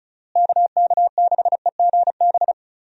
Morse code, heard as a keyed tone, about 34 wpm, 700 Hz. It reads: KK6EGB
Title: USA Call Signs - 6 Characters - Repeated 3x - 35wpm